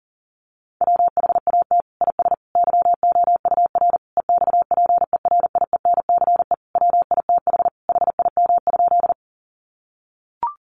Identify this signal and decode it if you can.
Morse code, heard as a keyed tone, about 37 wpm, 700 Hz. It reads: WHAT IS YOUR EXPERIENCE WITH HIM?
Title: Sentences from Top 600 Words 40wpm